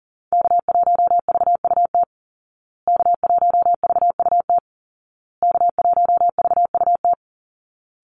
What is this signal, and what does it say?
Morse code, heard as a keyed tone, about 40 wpm, 700 Hz. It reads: K1VUT K1VUT K1VUT
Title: Call Signs Repeated 2x - 40wpm